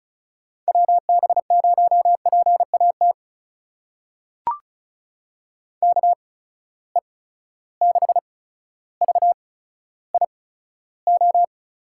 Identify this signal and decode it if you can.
Morse code, heard as a keyed tone, about 36 wpm, spaced as for 15 wpm, 700 Hz. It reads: WB0PAT KE6VIO
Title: USA Call Signs - 8x Character Spacing - 35wpm